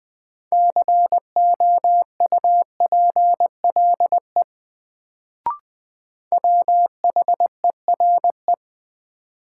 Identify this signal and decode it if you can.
Morse code, heard as a keyed tone, about 21 wpm, 700 Hz. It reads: COUPLE WHERE
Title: Sets of 2 Words - New Word Drill - 20wpm